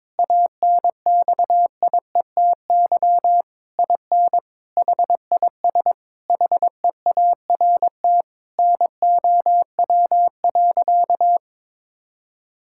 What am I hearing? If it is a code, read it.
Morse code, heard as a keyed tone, about 22 wpm, 700 Hz. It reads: ANXIETY IN HIS HEART NOW.
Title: Sentences from Top 1600 Words 22wpm